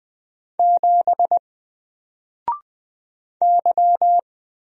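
Morse code, heard as a keyed tone, about 20 wpm, 700 Hz. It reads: 7 Y
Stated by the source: Single letter and number Morse drill at 20 wpm, from N4LCD.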